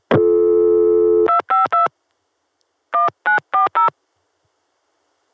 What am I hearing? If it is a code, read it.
DTMF, heard as telephone touch tones, tones of about 130 ms. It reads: 252 1940